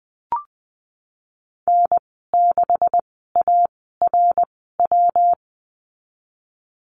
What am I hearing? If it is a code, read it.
Morse code, heard as a keyed tone, about 20 wpm, spaced as for 14 wpm, 700 Hz. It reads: N6ARW
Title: 128 Lesson 21g - Number 6 - Call Signs 20wpm